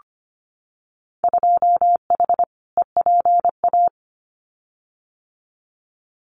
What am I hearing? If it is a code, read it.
Morse code, heard as a keyed tone, about 25 wpm, 700 Hz. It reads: 2H EPA
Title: Winter Field Day Exchanges 25wpm